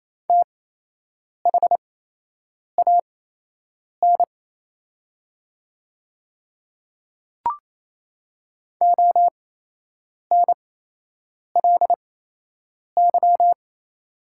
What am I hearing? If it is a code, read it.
Morse code, heard as a keyed tone, about 28 wpm, spaced as for 8 wpm, 700 Hz. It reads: THAN ONLY
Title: 8x Character Spacing - Top 200 Words - 28wpm